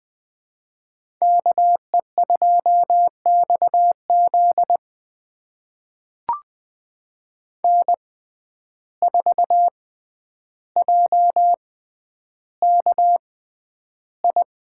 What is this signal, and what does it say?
Morse code, heard as a keyed tone, about 20 wpm, spaced as for 10 wpm, 700 Hz. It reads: KE2XZ N4JKI
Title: USA Call Signs - 5 Characters - 6x Character Spacing - 20wpm